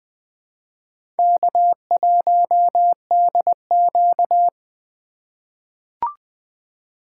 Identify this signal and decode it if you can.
Morse code, heard as a keyed tone, about 20 wpm, 700 Hz. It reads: K1DQ
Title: USA Call Signs - 2x Character Spacing - 20wpm